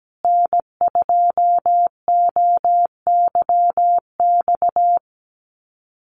Morse code, heard as a keyed tone, about 17 wpm, 700 Hz. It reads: N2OYX
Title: USA Call Signs Repeated 3x 17wpm